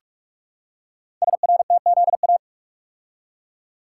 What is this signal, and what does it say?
Morse code, heard as a keyed tone, about 44 wpm, 700 Hz. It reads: SRTZA
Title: Code Groups - 5 Letters - 45wpm